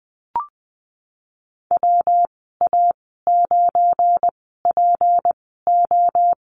Morse code, heard as a keyed tone, about 20 wpm, spaced as for 15 wpm, 700 Hz. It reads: WA9PO
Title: 200 Lesson 32g - Letter k - Call Signs 20wpm